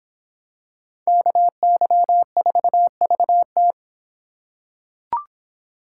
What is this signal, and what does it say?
Morse code, heard as a keyed tone, about 25 wpm, 700 Hz. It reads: KY4VT